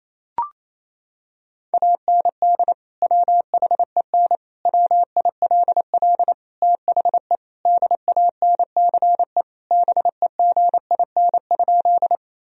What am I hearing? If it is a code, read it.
Morse code, heard as a keyed tone, about 27 wpm, 700 Hz. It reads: AND WHEN WILL THE DANCE BEGIN?
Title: Sentences from Top 1000 Words - Review Part 4 - 28wpm